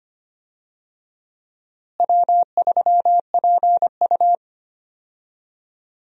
Morse code, heard as a keyed tone, about 25 wpm, 700 Hz. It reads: W3PU
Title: USA Call Signs - 4 Characters - Repeated 2x - 25wpm